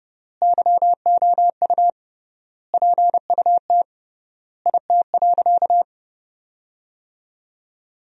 Morse code, heard as a keyed tone, about 30 wpm, 700 Hz. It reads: YOU PUT IT.
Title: 3x Word-Spacing - Sentences from Top 200 Words - 30wpm